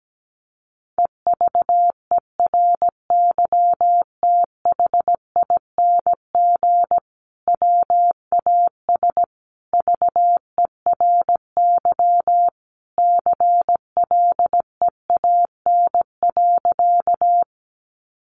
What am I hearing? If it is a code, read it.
Morse code, heard as a keyed tone, about 17 wpm, 700 Hz. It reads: EVERYTHING WAS VERY CLEAN.
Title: Sentences from Top 1100 Words 17wpm